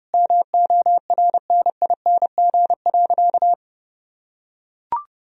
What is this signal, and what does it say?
Morse code, heard as a keyed tone, about 32 wpm, 700 Hz. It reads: MORNING.
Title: Sentences from Top 1000 Words - Review Part 2 - 30wpm